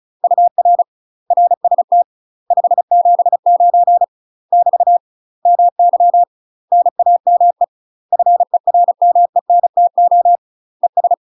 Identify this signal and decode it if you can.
Morse code, heard as a keyed tone, about 34 wpm, 700 Hz. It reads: UR RST 579 <BT> MY NAME FERMENTO ES
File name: UK MASTER QSOs TRAINING / 35WPM / UK MASTER QSOs 0015